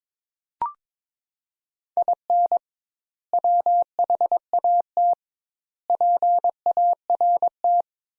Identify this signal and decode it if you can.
Morse code, heard as a keyed tone, about 22 wpm, 700 Hz. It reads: IN WHAT PART
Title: Sentences Top 200 Words - 22wpm